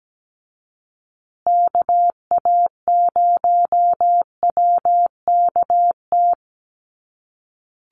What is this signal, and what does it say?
Morse code, heard as a keyed tone, about 17 wpm, 700 Hz. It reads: KA0WKT
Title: USA Call Signs Repeated 2x 17wpm